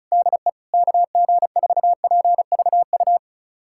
Morse code, heard as a keyed tone, about 36 wpm, 700 Hz. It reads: DE KG4PVU
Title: USA Callsign DE Callsign - 6 Characters - 35wpm